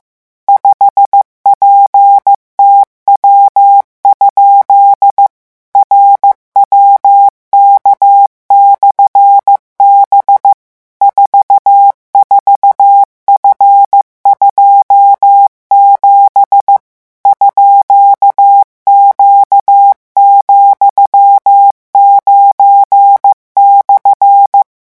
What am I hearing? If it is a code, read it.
Morse code, heard as a keyed tone, about 15 wpm, 800 Hz. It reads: 5PTW? RWK/B 44F27 _Q,9/
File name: morse mp3 / revision13-vitesse 14 mots